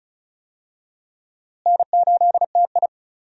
Morse code, heard as a keyed tone, about 36 wpm, 700 Hz. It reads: N8TI
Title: USA Call Signs - 4 Characters - Repeated 2x - 35wpm